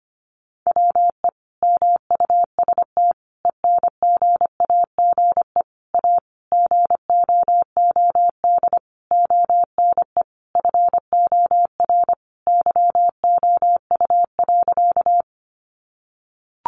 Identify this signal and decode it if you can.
Morse code, heard as a keyed tone, about 25 wpm, 700 Hz. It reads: WE MUST ENGAGE A GOOD ONE FOR YOU.
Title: Sentences from Top 1500 Words 25wpm